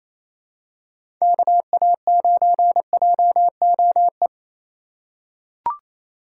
Morse code, heard as a keyed tone, about 28 wpm, 700 Hz. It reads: KA9JOE
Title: USA Call Signs - 2x Character Spacing - 28wpm